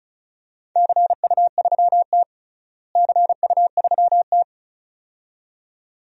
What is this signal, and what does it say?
Morse code, heard as a keyed tone, about 34 wpm, 700 Hz. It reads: CU3T CU3T
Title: DXCC-100 Call Signs - 4 Characters - Repeated 2x - 35wpm